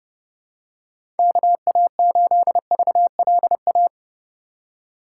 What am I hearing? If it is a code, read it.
Morse code, heard as a keyed tone, about 30 wpm, 700 Hz. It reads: KA8VLA
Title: USA Call Signs - 2x Character Spacing - 30wpm